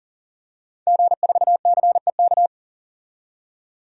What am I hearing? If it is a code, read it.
Morse code, heard as a keyed tone, about 40 wpm, 700 Hz. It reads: G4CEK
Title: DXCC-100 Call Signs - 5 Characters - 40wpm